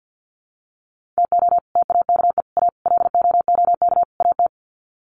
Morse code, heard as a keyed tone, about 50 wpm, 700 Hz. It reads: TO TAKE A LOOK AT
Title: Sets of 5 Words 50wpm